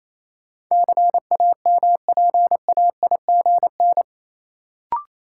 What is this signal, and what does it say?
Morse code, heard as a keyed tone, about 28 wpm, 700 Hz. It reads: CAMPAIGN
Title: Sentences from Top 1200 Words - New Word Drill - 28wpm